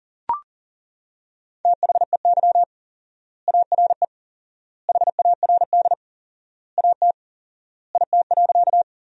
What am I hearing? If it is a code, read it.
Morse code, heard as a keyed tone, about 40 wpm, 700 Hz. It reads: THEY ARE HARD AT IT.
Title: 4x Word-Spacing - Sentences from Top 300 Words - 40wpm